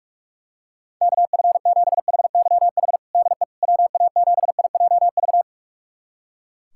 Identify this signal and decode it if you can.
Morse code, heard as a keyed tone, about 44 wpm, 700 Hz. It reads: KF7HYH DE WA7IJV
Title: USA Callsign DE Callsign - 6 Characters - 45wpm